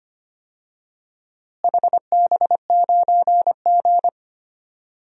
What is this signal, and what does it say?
Morse code, heard as a keyed tone, about 25 wpm, 700 Hz. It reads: HB9G
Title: DXCC-100 Call Signs - 4 Characters - Repeated 2x - 25wpm